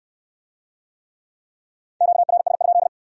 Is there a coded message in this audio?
Yes: Morse code from 2.0 to 2.9 s, heard as a keyed tone, about 63 wpm, 700 Hz. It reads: QRS?